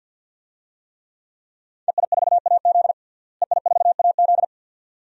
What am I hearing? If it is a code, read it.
Morse code, heard as a keyed tone, about 50 wpm, 700 Hz. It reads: EI4AZ EI4AZ
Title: DXCC-100 Call Signs - 5 Characters - Repeated 2x - 50wpm